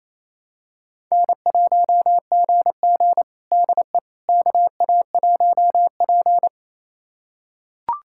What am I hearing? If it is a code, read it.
Morse code, heard as a keyed tone, about 28 wpm, 690 Hz. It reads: N1GG DE KA1P
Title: USA Callsign DE Callsign - 4 Characters - 28wpm